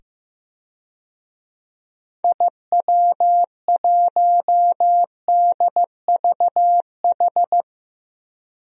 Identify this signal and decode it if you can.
Morse code, heard as a keyed tone, about 16 wpm, 700 Hz. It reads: IW1DVH
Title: DXCC-100 Call Signs - 6 Characters - 15wpm